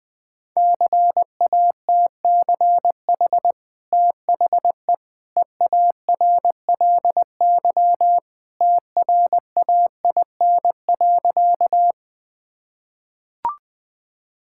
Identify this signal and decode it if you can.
Morse code, heard as a keyed tone, about 20 wpm, 700 Hz. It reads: CATCH THE EARLY TRAIN.
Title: Easy Sentences with Hints - 20wpm